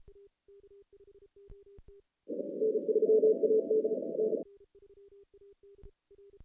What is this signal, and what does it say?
Morse code, heard as a keyed tone, about 36 wpm, 440 Hz. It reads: N3AD N
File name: 2016 JAN CW NAQP / 2016 Jan naqp k5ot - 0-10